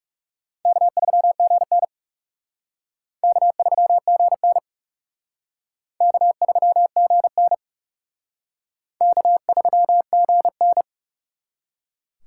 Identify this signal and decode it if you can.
Morse code, heard as a keyed tone, about 35 wpm, 700 Hz. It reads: K3GN K3GN K3GN K3GN
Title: Speed Racing - USA 1x2 Call Signs 30wpm